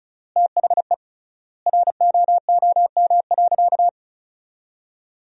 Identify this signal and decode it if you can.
Morse code, heard as a keyed tone, about 36 wpm, 700 Hz. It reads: THE ROOM.
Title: 3x Word-Spacing - Sentences from Top 500 Words - 35wpm